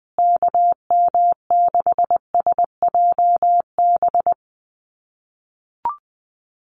Morse code, heard as a keyed tone, about 20 wpm, 700 Hz. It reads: KM6SJB
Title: USA Call Signs Repeated 3x 20wpm